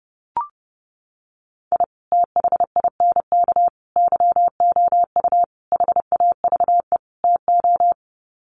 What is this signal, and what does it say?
Morse code, heard as a keyed tone, about 30 wpm, 700 Hz. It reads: I THINK YOU HAVE TO
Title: Sets of 5 Words 30wpm